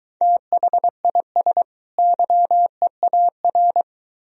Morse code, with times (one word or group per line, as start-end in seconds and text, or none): 0.2-1.6 THIS
2.0-3.8 YEAR